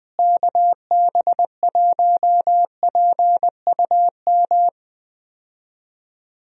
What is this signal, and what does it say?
Morse code, heard as a keyed tone, about 20 wpm, 700 Hz. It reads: KB1PUM
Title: USA Call Signs - 6 Characters - Repeated 2x - 20wpm